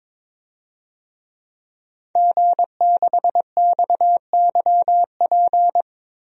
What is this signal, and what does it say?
Morse code, heard as a keyed tone, about 22 wpm, 700 Hz. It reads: G6XYP
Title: DXCC-100 Call Signs - 5 Characters - Repeated 3x - 22wpm